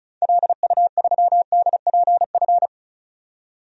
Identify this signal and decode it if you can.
Morse code, heard as a keyed tone, about 36 wpm, 700 Hz. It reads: LU3DPF
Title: DXCC-100 Call Signs - 6 Characters - 35wpm